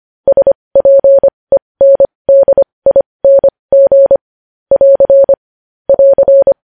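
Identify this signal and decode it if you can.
Morse code, heard as a keyed tone, about 25 wpm, spaced as for 20 wpm, 550 Hz. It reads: SPENDING <AR> <AR>
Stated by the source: Morse code news headlines sent at 20.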